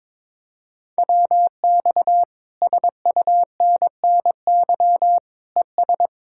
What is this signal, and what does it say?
Morse code, heard as a keyed tone, about 22 wpm, 700 Hz. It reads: WX SUNNY ES
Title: Speed-Racing Common QSO Phrases - 22wpm